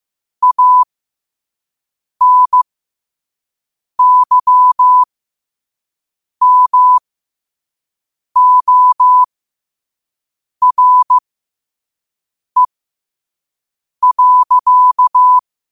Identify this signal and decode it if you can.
Morse code, heard as a keyed tone, about 15 wpm, spaced as for 5 wpm, 1000 Hz. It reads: ANYMORE.
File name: 5wpm